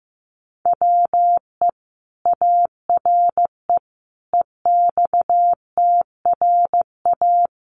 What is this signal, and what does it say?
Morse code, heard as a keyed tone, about 15 wpm, 700 Hz. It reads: WE ARE EXTRA
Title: Sentences from Top 1000 Words 15wpm